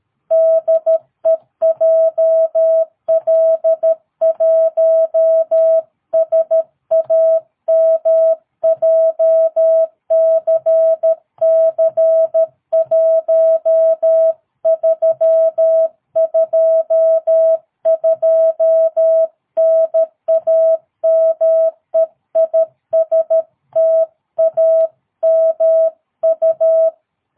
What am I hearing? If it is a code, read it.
Morse code, heard as a keyed tone, about 13 wpm, 650 Hz. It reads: DE JL1SAM JCC1322 NAME IS TAMU
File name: jh8cbh / 2022cw